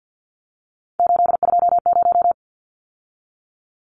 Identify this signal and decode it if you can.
Morse code, heard as a keyed tone, about 50 wpm, 700 Hz. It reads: 820
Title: Numbers - 3-Digit 50wpm